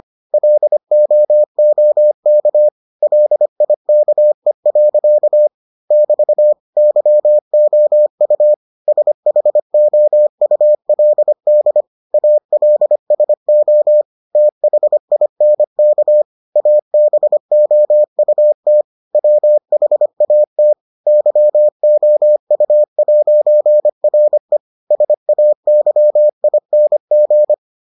Morse code, heard as a keyed tone, about 25 wpm, 600 Hz. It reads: LOOK LIKE. <BT> YOU SHOULD ALSO THINK ABOUT WHAT YOU'RE SAYING